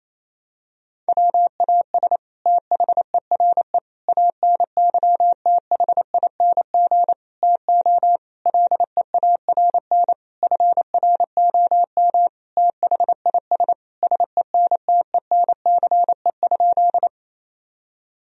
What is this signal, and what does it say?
Morse code, heard as a keyed tone, about 28 wpm, 700 Hz. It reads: WAS THERE ANYTHING TO LEARN FROM THIS SENTENCE?
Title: Sentences from Top 1000 Words - Review Part 1 - 28wpm